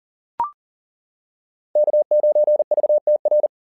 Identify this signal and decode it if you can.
Morse code, heard as a keyed tone, about 38 wpm, 660 Hz. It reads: K9VTR
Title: SOTA QSO Practice 40wpm